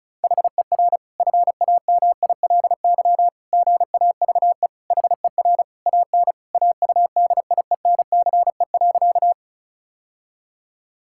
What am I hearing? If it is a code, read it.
Morse code, heard as a keyed tone, about 36 wpm, 700 Hz. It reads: HER FAMILY GAVE HER AN AUDIENCE.
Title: Sentences from Top 1000 Words - Review Part 5 - 35wpm